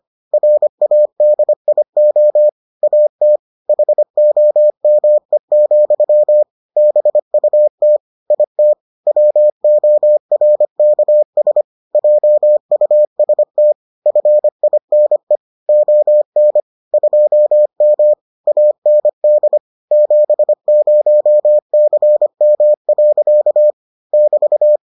Morse code, heard as a keyed tone, about 25 wpm, 600 Hz. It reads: RADIO AT HOME, BUT IT WORKS JUST FINE ON 2M AND 70CM. <BT>